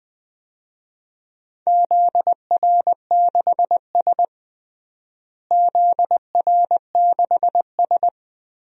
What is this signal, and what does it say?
Morse code, heard as a keyed tone, about 20 wpm, 700 Hz. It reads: ZR6S ZR6S
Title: DXCC-100 Call Signs - 4 Characters - Repeated 3x - 20wpm